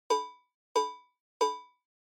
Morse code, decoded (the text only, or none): none